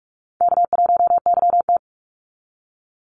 Morse code, heard as a keyed tone, about 44 wpm, 700 Hz. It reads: K1YT
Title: Easy Call Signs - 45wpm